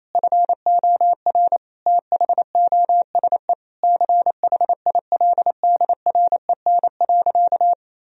Morse code, heard as a keyed tone, about 28 wpm, 700 Hz. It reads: FOR THOSE CHILDREN.
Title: Sentences from Top 1000 Words - Review Part 1 - 28wpm